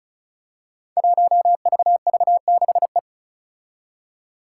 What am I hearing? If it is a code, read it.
Morse code, heard as a keyed tone, about 36 wpm, 700 Hz. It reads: 1VV6E
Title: Five Challenging Letter-Number Combinations 35wpm